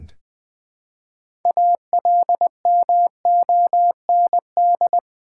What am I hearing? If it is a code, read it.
Morse code, heard as a keyed tone, about 20 wpm, 700 Hz. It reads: ALMOND
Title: Easy Words with Hints - 20wpm